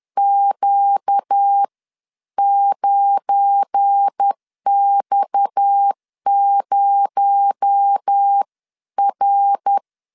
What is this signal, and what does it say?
Morse code, heard as a keyed tone, about 10 wpm, spaced as for 9 wpm, 800 Hz. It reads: Q9X0R